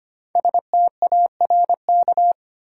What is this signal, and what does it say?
Morse code, heard as a keyed tone, about 25 wpm, 700 Hz. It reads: STARK
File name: 5 Letter Words - First 3 Letters Given - 25wpm